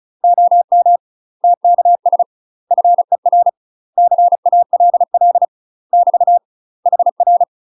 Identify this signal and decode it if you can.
Morse code, heard as a keyed tone, about 34 wpm, 700 Hz. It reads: OM TKS FER CALL <BT> HR
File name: UK MASTER QSOs TRAINING / 35WPM / UK MASTER QSOs 0042